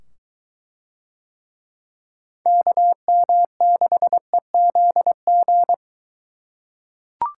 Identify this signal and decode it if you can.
Morse code, heard as a keyed tone, about 22 wpm, 700 Hz. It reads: KM6EZG